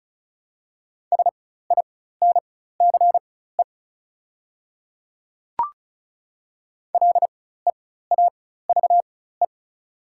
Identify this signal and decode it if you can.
Morse code, heard as a keyed tone, about 36 wpm, spaced as for 16 wpm, 700 Hz. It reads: SINCE LEAVE
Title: 4x Character Spacing - Top 500 Words - 35wpm